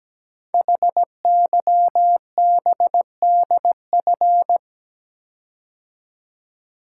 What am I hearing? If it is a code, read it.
Morse code, heard as a keyed tone, about 17 wpm, 700 Hz. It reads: HYBDF
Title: Code Groups - 5 Letter:Numbers - 17wpm